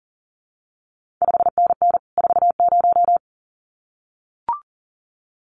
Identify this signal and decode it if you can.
Morse code, heard as a keyed tone, about 42 wpm, 700 Hz. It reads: 5NN 40
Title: CQ WW DX Contest Exchange Encore - 40wpm